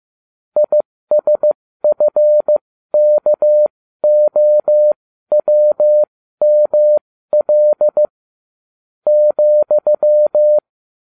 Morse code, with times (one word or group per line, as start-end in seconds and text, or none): 0.6-8.1 ISFKOWML
9.1-10.6 ,